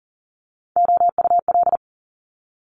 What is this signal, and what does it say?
Morse code, heard as a keyed tone, about 38 wpm, 700 Hz. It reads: OUL
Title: Top Three Letter Combinations in Words 40wpm